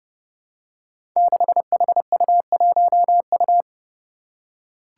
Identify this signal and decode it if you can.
Morse code, heard as a keyed tone, about 30 wpm, 700 Hz. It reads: 6HU1U
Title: Five Challenging Letter-Number Combinations 30wpm